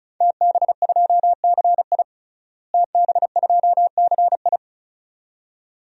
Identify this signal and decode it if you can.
Morse code, heard as a keyed tone, about 36 wpm, 700 Hz. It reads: TB2CI TB2CI
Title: DXCC-100 Call Signs - 5 Characters - Repeated 2x - 35wpm